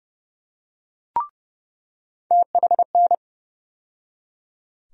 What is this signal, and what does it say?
Morse code, heard as a keyed tone, about 29 wpm, 700 Hz. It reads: THN
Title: USA Call Sign Postfixes - 3 Letters 30wpm